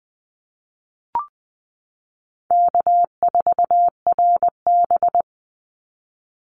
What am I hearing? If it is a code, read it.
Morse code, heard as a keyed tone, about 19 wpm, 700 Hz. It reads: K4RB
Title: Easy Call Signs Repeated 2x - 20wpm